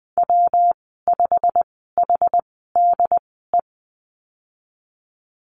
Morse code, heard as a keyed tone, about 20 wpm, spaced as for 14 wpm, 700 Hz. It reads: W5HDE
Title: 128 Lesson 21g - Number 6 - Call Signs 20wpm